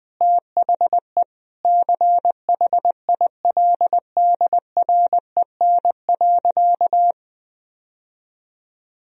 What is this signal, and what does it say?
Morse code, heard as a keyed tone, about 20 wpm, 700 Hz. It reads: THE CHILDREN.
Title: Sentences from Top 1000 Words - Review Part 3 - 20wpm